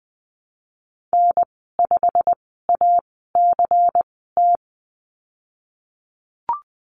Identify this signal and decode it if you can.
Morse code, heard as a keyed tone, about 20 wpm, spaced as for 15 wpm, 700 Hz. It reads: N5ACT
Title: 186 Lesson 30g - Letter b - Call Signs 20wpm